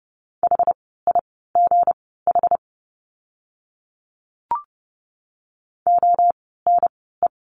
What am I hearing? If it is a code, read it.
Morse code, heard as a keyed tone, about 29 wpm, spaced as for 17 wpm, 700 Hz. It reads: HIGH ONE
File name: course / 30:10wpm / 237 Lesson 38f - Letter q - Words 30wpm